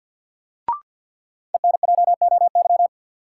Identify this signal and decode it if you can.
Morse code, heard as a keyed tone, about 53 wpm, 700 Hz. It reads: ENJOY
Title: Sentences from Top 1000 Words - Review Part 4 - New Word Drill - 50wpm